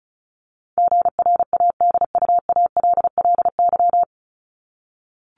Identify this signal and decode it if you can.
Morse code, heard as a keyed tone, about 34 wpm, 700 Hz. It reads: GRADUALLY
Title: LY Suffix - Top 100 Words with Base Words 35wpm